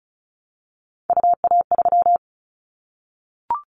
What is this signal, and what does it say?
Morse code, heard as a keyed tone, about 36 wpm, 690 Hz. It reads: UA3
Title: Callsign Prefixes for Top 50 DXCC - 35wpm